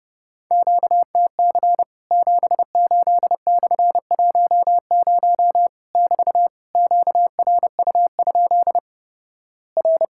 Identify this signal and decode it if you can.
Morse code, heard as a keyed tone, about 30 wpm, 700 Hz. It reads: QTC 78/10 <BT> QRU? R
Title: Worked All Europe DX Contest - QTC Exchange - 30wpm